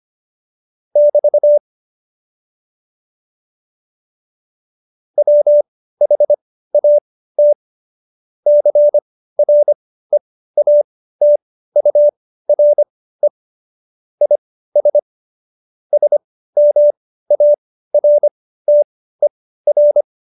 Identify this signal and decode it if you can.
Morse code, heard as a keyed tone, about 25 wpm, spaced as for 15 wpm, 590 Hz. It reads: <BT> WHAT CREATURE IS SMARTER
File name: Jokes-25-char-15-eff